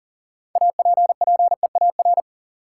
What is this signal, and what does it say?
Morse code, heard as a keyed tone, about 39 wpm, 700 Hz. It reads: APPEAR
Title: Sentences from Top 1000 Words - Review Part 2 - New Word Drill - 40wpm